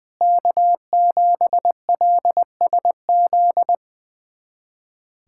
Morse code, heard as a keyed tone, about 20 wpm, 700 Hz. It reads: K7LSZ
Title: USA Callsign DE Callsign - 5 Characters - 20wpm